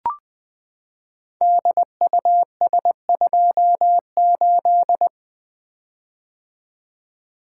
Morse code, heard as a keyed tone, about 21 wpm, 700 Hz. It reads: DUS28
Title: Code Groups - 5 Letter:Numbers - 20wpm